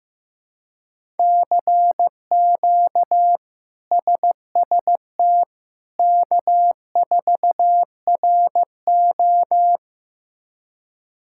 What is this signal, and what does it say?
Morse code, heard as a keyed tone, about 15 wpm, 700 Hz. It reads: CQ SST K4RO
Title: K1USN SST QSO Practice - 15wpm